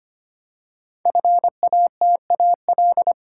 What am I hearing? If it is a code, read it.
Morse code, heard as a keyed tone, about 25 wpm, 700 Hz. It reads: FATAL
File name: 5 Letter Words - First 3 Letters Given - 25wpm